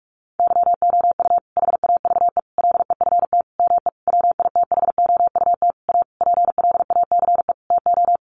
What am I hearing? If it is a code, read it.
Morse code, heard as a keyed tone, about 44 wpm, 700 Hz. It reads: YOU HAVE LEFT ME WITHOUT A PLACE TO
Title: Sentences from Top 600 Words 45wpm